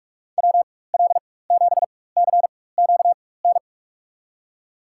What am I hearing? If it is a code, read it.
Morse code, heard as a keyed tone, about 42 wpm, spaced as for 20 wpm, 700 Hz. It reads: WL7CQN
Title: USA Call Signs - 6 Characters - 4x Character Spacing - 45wpm